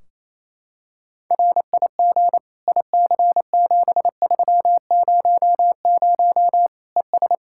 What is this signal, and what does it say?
Morse code, heard as a keyed tone, about 28 wpm, 700 Hz. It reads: RIG IC7300 ES
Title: Practice QSO - Exchange 2 - Info - 28wpm